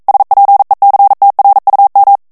Morse code, heard as a keyed tone, about 42 wpm, 800 Hz. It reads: SPECTRUM